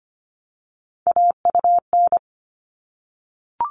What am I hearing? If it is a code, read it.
Morse code, heard as a keyed tone, about 25 wpm, 700 Hz. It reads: AUN
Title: All Cut Numbers - 3-Digit 25wpm